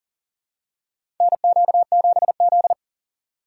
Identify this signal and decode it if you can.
Morse code, heard as a keyed tone, about 40 wpm, 700 Hz. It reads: NQ7Z
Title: USA Call Signs - 4 Characters - 2x Character Spacing - 40wpm